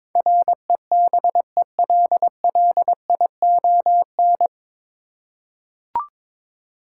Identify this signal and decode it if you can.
Morse code, heard as a keyed tone, about 22 wpm, 700 Hz. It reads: REBELLION
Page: Morse Code Ninja practice 22wpm